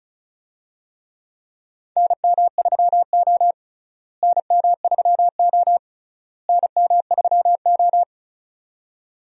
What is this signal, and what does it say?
Morse code, heard as a keyed tone, about 36 wpm, 700 Hz. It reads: NM3O NM3O NM3O
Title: USA Call Signs - 4 Characters - Repeated 3x - 35wpm